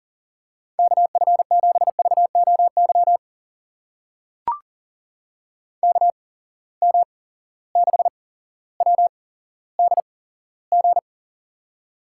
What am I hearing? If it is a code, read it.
Morse code, heard as a keyed tone, about 40 wpm, spaced as for 17 wpm, 700 Hz. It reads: KF7VOY KM6WDG
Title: USA Call Signs - 6 Characters - 8x Character Spacing - 40wpm